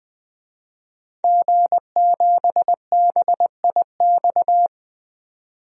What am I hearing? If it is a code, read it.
Morse code, heard as a keyed tone, about 20 wpm, 700 Hz. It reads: G7BIX
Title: DXCC-100 Call Signs - 5 Characters - 20wpm